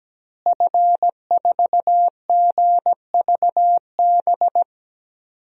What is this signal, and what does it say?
Morse code, heard as a keyed tone, about 17 wpm, 700 Hz. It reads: F4GVB
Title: DXCC-100 Call Signs - 5 Characters - Repeated 3x - 17wpm